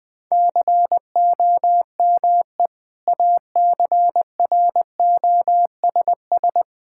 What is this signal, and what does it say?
Morse code, heard as a keyed tone, about 20 wpm, 700 Hz. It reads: COME ACROSS
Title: Sentences from Top 1000 Words - Review Part 5 - 20wpm